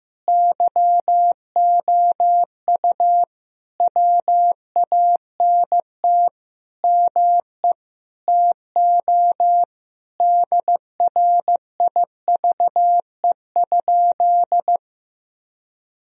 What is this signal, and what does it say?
Morse code, heard as a keyed tone, about 15 wpm, 700 Hz. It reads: YOU WANT ME TO DRIVE?
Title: Sentences from Top 1000 Words - Review Part 2 - 15wpm